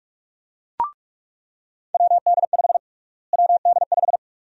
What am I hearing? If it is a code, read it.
Morse code, heard as a keyed tone, about 46 wpm, 700 Hz. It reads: WD5 WD5
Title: USA Call Sign Prefixes - Repeated 2x - 45wpm